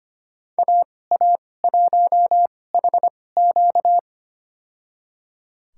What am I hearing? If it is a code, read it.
Morse code, heard as a keyed tone, about 25 wpm, spaced as for 18 wpm, 700 Hz. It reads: AA1HQ
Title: USA Call Signs - 5 Characters - 2x Character Spacing - 25wpm